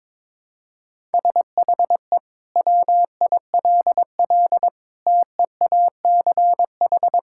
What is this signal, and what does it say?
Morse code, heard as a keyed tone, about 22 wpm, 700 Hz. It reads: SHE WILL TEACH
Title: Sentences from Top 1000 Words - Review Part 6 - 22wpm